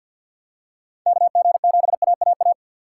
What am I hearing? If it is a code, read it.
Morse code, heard as a keyed tone, about 52 wpm, 700 Hz. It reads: KG7AAA
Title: USA Call Signs - 6 Characters - 6x Character Spacing - 50wpm